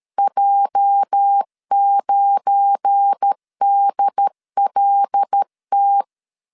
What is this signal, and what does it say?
Morse code, heard as a keyed tone, about 13 wpm, 800 Hz. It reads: J9DLT
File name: LEZIONI / 8 YJ45 / 8 YJ4 12wpm 2151